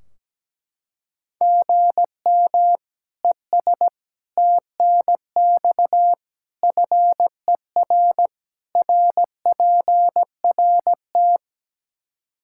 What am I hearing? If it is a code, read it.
Morse code, heard as a keyed tone, about 17 wpm, 700 Hz. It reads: GM ES TNX FER RPRT
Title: Practice QSO - Exchange 1 - Beginning - 17wpm